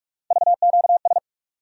Morse code, heard as a keyed tone, about 42 wpm, 700 Hz. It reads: VQS
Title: USA Call Sign Postfixes - 3 Letters 45wpm